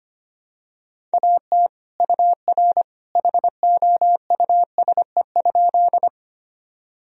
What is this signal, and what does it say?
Morse code, heard as a keyed tone, about 25 wpm, 700 Hz. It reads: AT UR HOUSE?
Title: Sentences from Top 200 Words and Abbreviations 25wpm